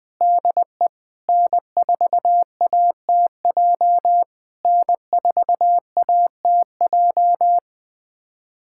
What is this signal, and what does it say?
Morse code, heard as a keyed tone, about 20 wpm, 700 Hz. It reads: DE N4ATJ N4ATJ
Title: POTA QSO Practice - 20wpm